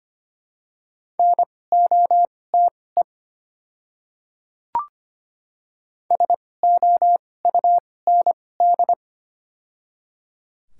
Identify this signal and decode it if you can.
Morse code, heard as a keyed tone, about 25 wpm, spaced as for 18 wpm, 700 Hz. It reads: NOTE SOUND
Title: 2x Character Spacing - Top 500 Words - 25wpm